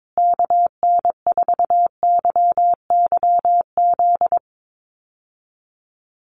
Morse code, heard as a keyed tone, about 22 wpm, 700 Hz. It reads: KN4YYZ
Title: USA Call Signs Repeated 3x 22wpm